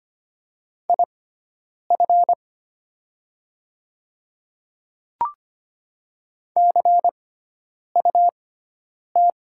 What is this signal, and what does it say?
Morse code, heard as a keyed tone, about 25 wpm, spaced as for 9 wpm, 700 Hz. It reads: IF CUT
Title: 6x Character Spacing - Top 300 Words - 25wpm